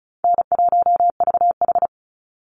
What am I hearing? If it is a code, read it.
Morse code, heard as a keyed tone, about 34 wpm, 700 Hz. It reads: N1VH
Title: Easy Call Signs Repeated 2x - 35wpm